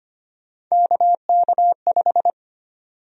Morse code, heard as a keyed tone, about 25 wpm, 700 Hz. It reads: KK5
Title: USA Call Sign Prefixes - Repeated 2x - 25wpm